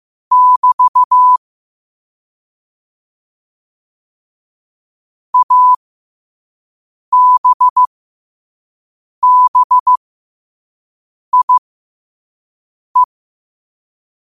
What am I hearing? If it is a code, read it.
Morse code, heard as a keyed tone, about 15 wpm, spaced as for 5 wpm, 1000 Hz. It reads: <BT> ABBIE
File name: qotdfiles / 5wpm